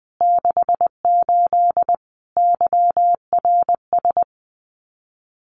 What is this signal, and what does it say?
Morse code, heard as a keyed tone, about 20 wpm, 700 Hz. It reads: 68 YRS
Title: Common QSO Phrases Encore 20wpm